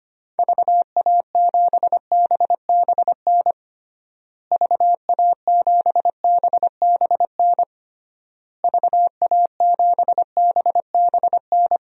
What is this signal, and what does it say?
Morse code, heard as a keyed tone, about 25 wpm, 700 Hz. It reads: VA7BBN VA7BBN VA7BBN
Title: DXCC-100 Call Signs - 6 Characters - Repeated 3x - 25wpm